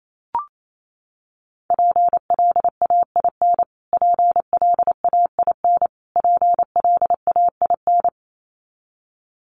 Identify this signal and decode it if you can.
Morse code, heard as a keyed tone, about 28 wpm, 700 Hz. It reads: PLAIN PLAIN PLAIN
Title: Top 500 Words Repeated 28wpm